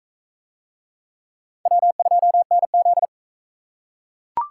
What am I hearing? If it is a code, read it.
Morse code, heard as a keyed tone, about 37 wpm, 700 Hz. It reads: W2NZ